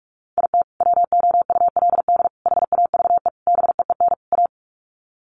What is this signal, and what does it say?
Morse code, heard as a keyed tone, about 44 wpm, 700 Hz. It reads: IT WOULD HAVE BEEN A
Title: Sets of 5 Words Encore 45wpm